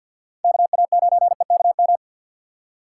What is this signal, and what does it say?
Morse code, heard as a keyed tone, about 50 wpm, 700 Hz. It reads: KA9EKM
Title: USA Call Signs - 2x Character Spacing - 50wpm